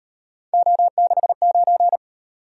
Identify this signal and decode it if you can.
Morse code, heard as a keyed tone, about 38 wpm, 700 Hz. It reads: O69